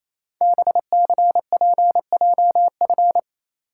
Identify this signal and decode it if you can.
Morse code, heard as a keyed tone, about 27 wpm, 700 Hz. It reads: BCPJF
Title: Code Groups - 5 Letters - 28wpm